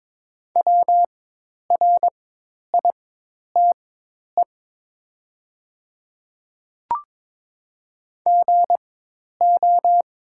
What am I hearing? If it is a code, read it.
Morse code, heard as a keyed tone, about 22 wpm, spaced as for 10 wpm, 700 Hz. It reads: WRITE GO
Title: 4x Character Spacing - Top 100 Words - 22wpm